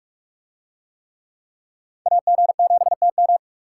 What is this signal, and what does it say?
Morse code, heard as a keyed tone, about 44 wpm, 700 Hz. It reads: AG7TM